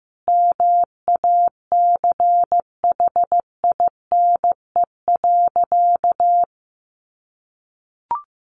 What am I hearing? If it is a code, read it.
Morse code, heard as a keyed tone, about 15 wpm, 700 Hz. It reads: MACHINE.
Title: A Sound Of Thunder 15wpm